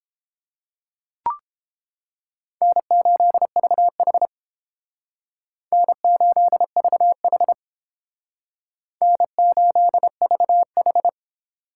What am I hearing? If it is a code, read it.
Morse code, heard as a keyed tone, about 30 wpm, 700 Hz. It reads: N8VH N8VH N8VH